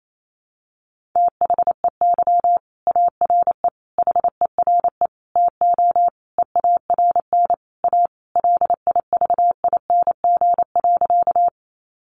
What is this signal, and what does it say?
Morse code, heard as a keyed tone, about 28 wpm, 700 Hz. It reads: THEY ARE HERE TO EARN A LIVING.
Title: Sentences from Top 1500 Words 28wpm